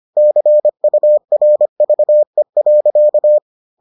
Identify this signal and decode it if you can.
Morse code, heard as a keyed tone, about 25 wpm, 600 Hz. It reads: CURVE.